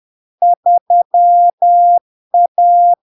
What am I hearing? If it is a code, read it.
Morse code, heard as a keyed tone, about 10 wpm, 700 Hz. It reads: 3A
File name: LARK-FD-2022-Log-in-CW-10wpm-700-Hz